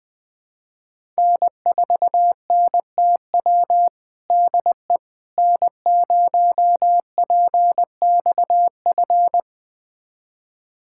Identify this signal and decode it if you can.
Morse code, heard as a keyed tone, about 20 wpm, 700 Hz. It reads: N4NTW DE N0PXF
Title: USA Callsign DE Callsign - 5 Characters - 20wpm